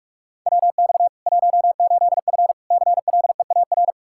Morse code, heard as a keyed tone, about 42 wpm, 700 Hz. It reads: WX 18F CLEAR
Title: Practice QSO - Exchange 2 - Info - 45wpm